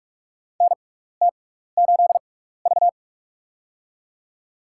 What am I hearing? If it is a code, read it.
Morse code, heard as a keyed tone, about 42 wpm, spaced as for 15 wpm, 700 Hz. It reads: NT8V
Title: USA Call Signs - 4 Characters - 6x Character Spacing - 45wpm